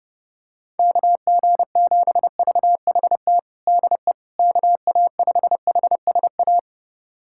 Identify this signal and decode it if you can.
Morse code, heard as a keyed tone, about 30 wpm, 700 Hz. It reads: KG7VHT DE KA5HSA
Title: USA Callsign DE Callsign - 6 Characters - 30wpm